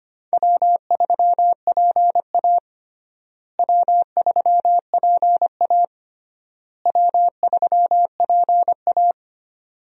Morse code, heard as a keyed tone, about 25 wpm, 700 Hz. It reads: W3PA W3PA W3PA
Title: USA Call Signs - 4 Characters - Repeated 3x - 25wpm